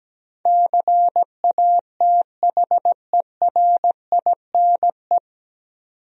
Morse code, heard as a keyed tone, about 17 wpm, 700 Hz. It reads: CATHERINE
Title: USA Common Names - 17wpm